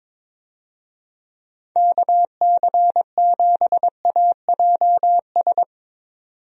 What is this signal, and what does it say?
Morse code, heard as a keyed tone, about 22 wpm, 700 Hz. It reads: KC7AJS
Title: USA Call Signs - 6 Characters - 22wpm